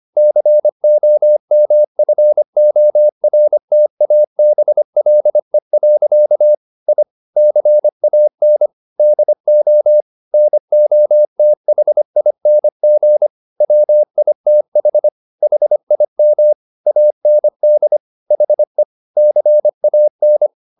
Morse code, heard as a keyed tone, about 25 wpm, 600 Hz. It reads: COMFORTABLE. I CAN DO NOTHING WITH HIM AND HE CAN
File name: the old man and the sea part 1 25wpm